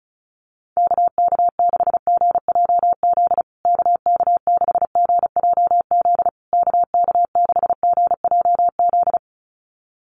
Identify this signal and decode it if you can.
Morse code, heard as a keyed tone, about 34 wpm, 700 Hz. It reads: KK6GJZ KK6GJZ KK6GJZ
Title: USA Call Signs Repeated 3x 35wpm